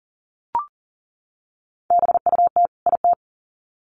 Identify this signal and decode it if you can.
Morse code, heard as a keyed tone, about 40 wpm, 700 Hz. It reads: BUT IT
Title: Sets of 2 Words 40wpm